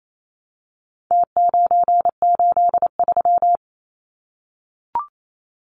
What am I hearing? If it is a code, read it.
Morse code, heard as a keyed tone, about 28 wpm, 700 Hz. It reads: T983
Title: Numbers - Contest Style - 4-Digit 28wpm